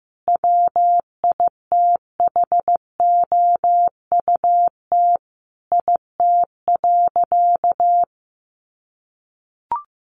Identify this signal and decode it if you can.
Morse code, heard as a keyed tone, about 15 wpm, 700 Hz. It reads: WITHOUT IT.
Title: Sentences from Top 600 Words 15wpm